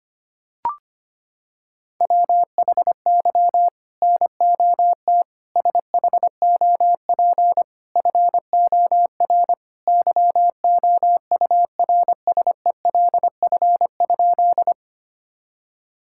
Morse code, heard as a keyed tone, about 25 wpm, 700 Hz. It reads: WHY NOT SHOP FOR YOURSELF?
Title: Sentences from Top 1600 Words 25wpm